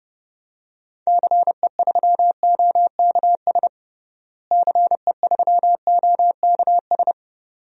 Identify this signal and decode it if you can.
Morse code, heard as a keyed tone, about 29 wpm, 700 Hz. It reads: CE3OKS CE3OKS
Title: DXCC-100 Call Signs - 6 Characters - Repeated 2x - 30wpm